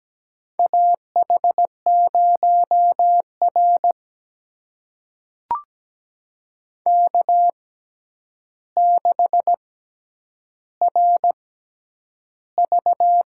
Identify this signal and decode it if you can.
Morse code, heard as a keyed tone, about 17 wpm, spaced as for 9 wpm, 700 Hz. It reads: AH0R K6RV
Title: USA Call Signs - 4 Characters - 6x Character Spacing - 17wpm